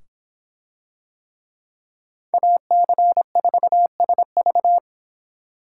Morse code, heard as a keyed tone, about 24 wpm, 700 Hz. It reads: AC4SV